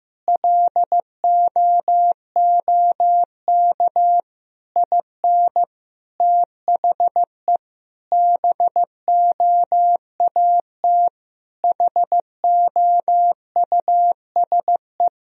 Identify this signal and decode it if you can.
Morse code, heard as a keyed tone, about 15 wpm, 700 Hz. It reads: LOOK IN THE BOAT HOUSE
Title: Sentences from Top 1000 Words - Review Part 3 - 15wpm